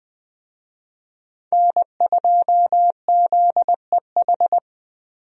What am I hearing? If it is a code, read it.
Morse code, heard as a keyed tone, about 21 wpm, 700 Hz. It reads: N2ZEH
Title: USA Call Signs - 5 Characters - Repeated 2x - 20wpm